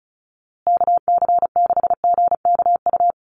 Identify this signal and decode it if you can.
Morse code, heard as a keyed tone, about 34 wpm, 700 Hz. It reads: KC6GKU
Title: USA Call Signs Repeated 3x 35wpm